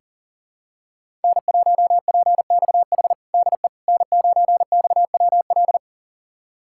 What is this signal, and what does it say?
Morse code, heard as a keyed tone, about 40 wpm, 700 Hz. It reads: N1PXH DE N9XWL
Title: USA Callsign DE Callsign - 5 Characters - 40wpm